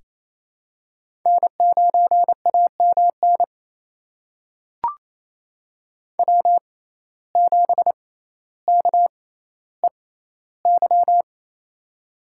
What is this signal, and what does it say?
Morse code, heard as a keyed tone, about 27 wpm, spaced as for 14 wpm, 700 Hz. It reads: N9AMN W7KEY
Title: USA Call Signs - 5 Characters - 6x Character Spacing - 28wpm